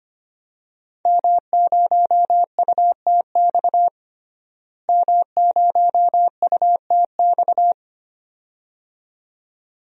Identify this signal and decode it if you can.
Morse code, heard as a keyed tone, about 25 wpm, 700 Hz. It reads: M0UTX M0UTX
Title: DXCC-100 Call Signs - 5 Characters - Repeated 2x - 25wpm